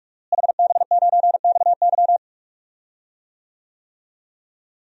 Morse code, heard as a keyed tone, about 46 wpm, 700 Hz. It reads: HB9XY
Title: DXCC-100 Call Signs - 5 Characters - Repeated 3x - 45wpm